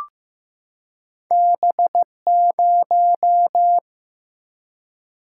Morse code, heard as a keyed tone, about 15 wpm, 700 Hz. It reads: B0
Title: Single-Letter Single-Number 15wpm